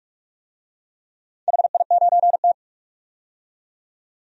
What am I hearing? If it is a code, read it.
Morse code, heard as a keyed tone, about 44 wpm, 700 Hz. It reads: HI9T
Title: DXCC-100 Call Signs - 4 Characters - 45wpm